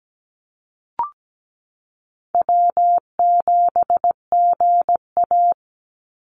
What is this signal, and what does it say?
Morse code, heard as a keyed tone, about 17 wpm, 700 Hz. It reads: W7GA
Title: NA SOTA Chaser Callsigns 17wpm